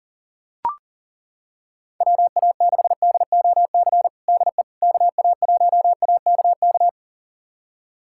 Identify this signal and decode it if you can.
Morse code, heard as a keyed tone, about 40 wpm, 700 Hz. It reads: WA6DOC DE KA1AKK
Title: USA Callsign DE Callsign - 6 Characters - 40wpm